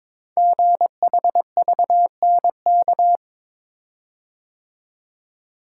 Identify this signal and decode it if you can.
Morse code, heard as a keyed tone, about 23 wpm, 700 Hz. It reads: GHVNK
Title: Code Groups - 5 Letters - 22wpm